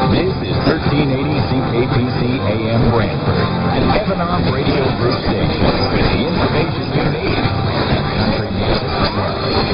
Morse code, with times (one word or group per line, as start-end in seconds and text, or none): none